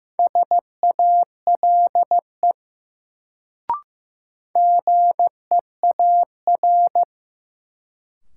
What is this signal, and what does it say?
Morse code, heard as a keyed tone, about 15 wpm, 700 Hz. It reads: SALE GEAR
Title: Sentences from Top 1400 Words - New Word Drill - 15wpm